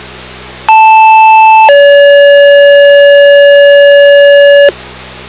Two-tone page, heard as a one-second tone, then a three-second tone, about 880 then 570 Hz, with noise. It is heard from 0.7 to 4.7 s.
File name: Sumner Ambulance